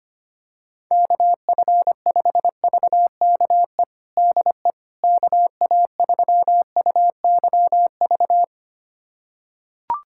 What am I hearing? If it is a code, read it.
Morse code, heard as a keyed tone, about 25 wpm, 700 Hz. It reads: KF5VKE DE KA3UYV
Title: USA Callsign DE Callsign - 6 Characters - 25wpm